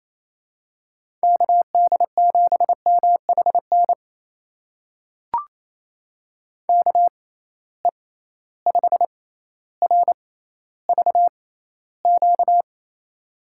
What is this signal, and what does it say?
Morse code, heard as a keyed tone, about 28 wpm, spaced as for 14 wpm, 700 Hz. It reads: KD7MHN KE5RVQ
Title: USA Call Signs - 6 Characters - 6x Character Spacing - 28wpm